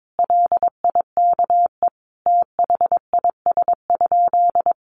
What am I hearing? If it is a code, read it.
Morse code, heard as a keyed tone, about 22 wpm, 700 Hz. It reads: LIKE THIS?
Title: Sentences from Top 1200 Words 22wpm